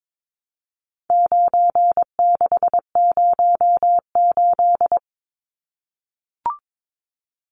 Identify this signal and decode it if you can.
Morse code, heard as a keyed tone, about 22 wpm, 700 Hz. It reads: 9608
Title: Numbers - 4-Digit 22wpm